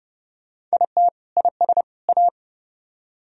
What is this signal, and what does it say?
Morse code, heard as a keyed tone, about 39 wpm, 700 Hz. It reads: IT IS A